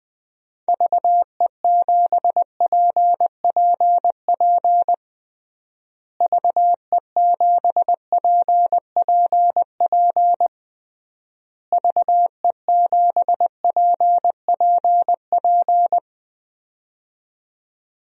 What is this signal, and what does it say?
Morse code, heard as a keyed tone, about 21 wpm, 700 Hz. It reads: VE7PPP VE7PPP VE7PPP
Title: DXCC-100 Call Signs - 6 Characters - Repeated 3x - 20wpm